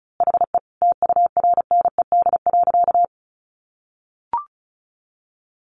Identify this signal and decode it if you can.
Morse code, heard as a keyed tone, about 34 wpm, 700 Hz. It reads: HE TURNED.
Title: A Sound Of Thunder 35wpm